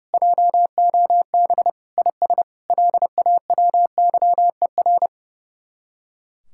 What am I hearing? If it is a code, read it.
Morse code, heard as a keyed tone, about 39 wpm, 700 Hz. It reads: JOB IS LAWYER